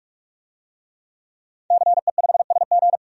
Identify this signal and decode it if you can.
Morse code, heard as a keyed tone, about 42 wpm, 700 Hz. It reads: CE5SG
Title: DXCC-100 Call Signs - 5 Characters - 45wpm